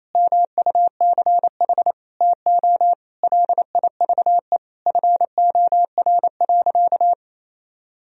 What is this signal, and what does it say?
Morse code, heard as a keyed tone, about 28 wpm, 700 Hz. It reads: MUCH TO LIVE FOR.
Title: Sentences from Top 1000 Words - Review Part 3 - 28wpm